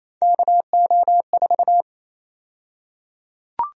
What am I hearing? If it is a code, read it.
Morse code, heard as a keyed tone, about 28 wpm, 700 Hz. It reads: KO4
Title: USA Call Sign Prefixes - Repeated 2x - 28wpm